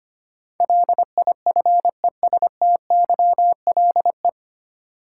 Morse code, heard as a keyed tone, about 25 wpm, 700 Hz. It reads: LIFESTYLE
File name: Nine Letter Words - 25wpm